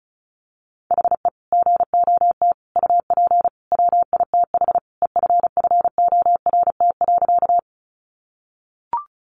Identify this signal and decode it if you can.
Morse code, heard as a keyed tone, about 34 wpm, 700 Hz. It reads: HE GOT UP WITH EFFORT.
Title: Sentences from Top 1200 Words 35wpm